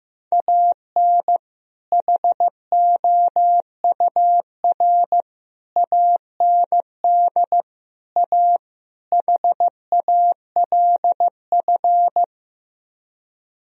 Morse code, heard as a keyed tone, about 16 wpm, 700 Hz. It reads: AN HOUR AND A HALF
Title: Speed-Racing - Sets of 5 Words 15wpm